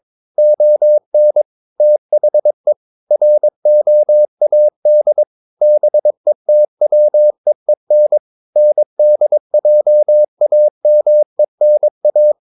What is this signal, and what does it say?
Morse code, heard as a keyed tone, about 22 wpm, 600 Hz. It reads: ON THE ROAD BETWEEN NDJAMENA
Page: Romanen Never af Ken Follet på engelsk.